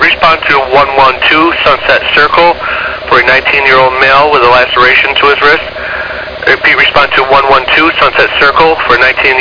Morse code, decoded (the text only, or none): none